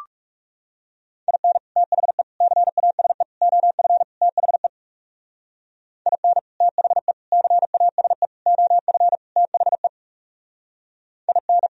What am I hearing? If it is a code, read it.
Morse code, heard as a keyed tone, about 38 wpm, 700 Hz. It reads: IN THE CASE OF THE IN THE CASE OF THE IN